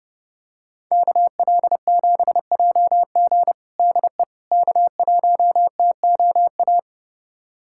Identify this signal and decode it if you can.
Morse code, heard as a keyed tone, about 30 wpm, 700 Hz. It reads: KL7JG DE K1TOA
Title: USA Callsign DE Callsign - 5 Characters - 30wpm